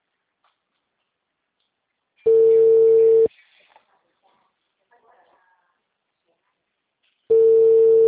Ringback tone (China, 450 Hz).